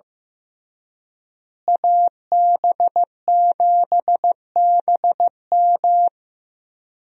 Morse code, heard as a keyed tone, about 15 wpm, 700 Hz. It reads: AB7BM
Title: USA Call Signs - 5 Characters - Repeated 3x - 15wpm